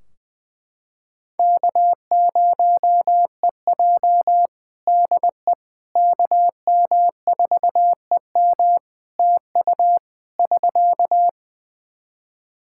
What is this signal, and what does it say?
Morse code, heard as a keyed tone, about 20 wpm, 700 Hz. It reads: K0EJ DE KM4EM TU <SK>
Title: Practice QSO - Exchange 4 - Ending - 20wpm